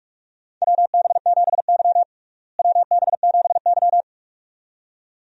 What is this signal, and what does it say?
Morse code, heard as a keyed tone, about 41 wpm, 700 Hz. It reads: WB7Y WB7Y
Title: USA Call Signs - 4 Characters - Repeated 2x - 45wpm